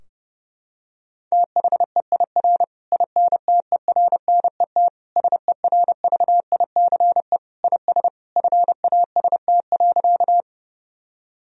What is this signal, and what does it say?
Morse code, heard as a keyed tone, about 28 wpm, 700 Hz. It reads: THEIR INTERNET SERVICE IS FAST.
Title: Sentences from Top 1000 Words - Review Part 4 - 30wpm